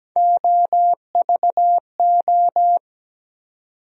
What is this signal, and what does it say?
Morse code, heard as a keyed tone, about 17 wpm, 700 Hz. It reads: OVO
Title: USA Call Sign Postfixes - 3 Letters 17wpm